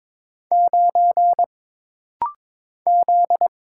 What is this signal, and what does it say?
Morse code, heard as a keyed tone, about 22 wpm, 700 Hz. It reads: 9 Z
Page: This rapid-fire timing is geared to push your ICR (Instant Character Recognition) skills.